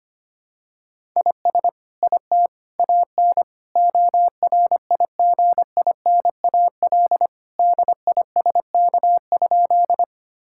Morse code, heard as a keyed tone, about 25 wpm, 700 Hz. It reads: IS IT AN ORIGINAL DISK?
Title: Sentences from Top 1000 Words - Review Part 6 - 25wpm